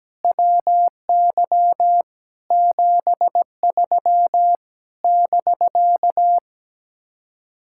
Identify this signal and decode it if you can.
Morse code, heard as a keyed tone, about 17 wpm, 700 Hz. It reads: WY 73 <BK>
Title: Speed-Racing Common QSO Phrases - 17wpm